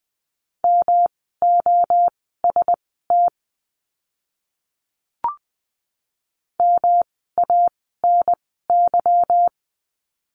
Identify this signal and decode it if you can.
Morse code, heard as a keyed tone, about 20 wpm, spaced as for 15 wpm, 700 Hz. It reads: MOST MANY
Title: 206 Lesson 33f - Letter j - Words 20wpm